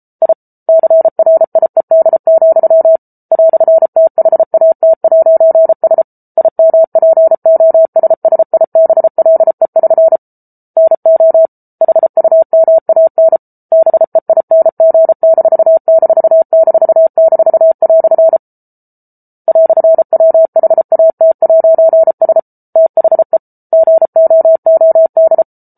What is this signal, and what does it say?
Morse code, heard as a keyed tone, about 34 wpm, 650 Hz. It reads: I CRIED, "THAT'S IMPOSSIBLE<SN> NO HUMAN BEING----" "WHAT'S THE GOOD